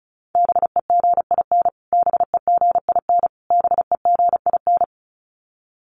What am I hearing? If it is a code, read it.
Morse code, heard as a keyed tone, about 34 wpm, 700 Hz. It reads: BEGIN BEGIN BEGIN
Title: Top 300 Words Repeated 35wpm